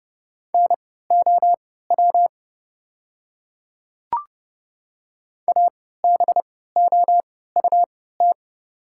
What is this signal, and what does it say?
Morse code, heard as a keyed tone, about 30 wpm, spaced as for 17 wpm, 700 Hz. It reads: NOW ABOUT
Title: 3x Character Spacing - Top 100 Words - 30wpm